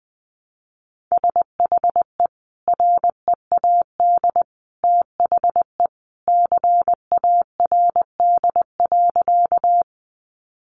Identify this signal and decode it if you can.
Morse code, heard as a keyed tone, about 20 wpm, 700 Hz. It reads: SHE READ THE CARD.
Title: Sentences from Top 600 Words 20wpm